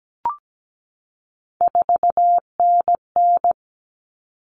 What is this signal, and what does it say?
Morse code, heard as a keyed tone, about 17 wpm, 700 Hz. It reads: 4NN
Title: All Cut Numbers - 3-Digit 17wpm